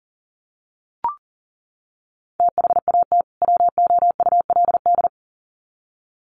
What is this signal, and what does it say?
Morse code, heard as a keyed tone, about 40 wpm, 700 Hz. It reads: THAT WOULD
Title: Sets of 2 Words 40wpm